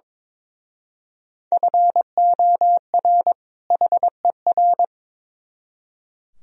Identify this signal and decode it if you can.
Morse code, heard as a keyed tone, about 26 wpm, 700 Hz. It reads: FOR HER
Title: Speed-Racing - Sets of 2 Words 22wpm